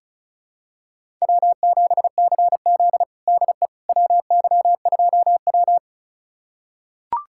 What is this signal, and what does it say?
Morse code, heard as a keyed tone, about 36 wpm, 700 Hz. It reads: W7CZ DE WY2W
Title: USA Callsign DE Callsign - 4 Characters - 35wpm